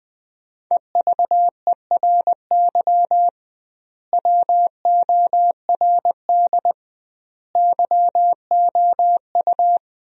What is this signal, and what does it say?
Morse code, heard as a keyed tone, about 20 wpm, 700 Hz. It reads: EVERY WORD YOU
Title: 2x Word-Spacing - Sentences from Top 400 Words - 20wpm